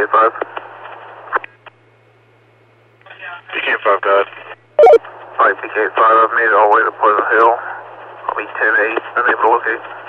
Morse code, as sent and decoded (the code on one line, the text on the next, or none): none